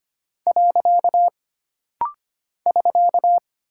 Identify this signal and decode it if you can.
Morse code, heard as a keyed tone, about 25 wpm, 700 Hz. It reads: . <SK>
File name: Single Punctuation-Prosign - Rapid-Fire - 25wpm